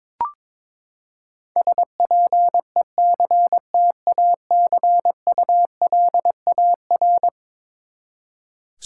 Morse code, heard as a keyed tone, about 21 wpm, 700 Hz. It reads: SPECTACULAR
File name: practice / 22wpm / Eleven Letter Words - 22wpm